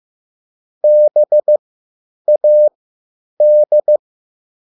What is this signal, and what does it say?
Morse code, heard as a keyed tone, about 15 wpm, spaced as for 9 wpm, 600 Hz. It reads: BAD